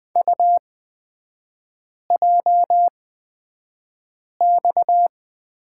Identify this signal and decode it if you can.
Morse code, heard as a keyed tone, about 20 wpm, spaced as for 5 wpm, 690 Hz. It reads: UJX
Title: 229 Lesson 37e - Letter x - Three Characters 20wpm